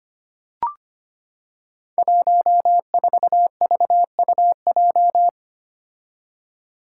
Morse code, heard as a keyed tone, about 25 wpm, 700 Hz. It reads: 14VUJ
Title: Five Challenging Letter-Number Combinations 25wpm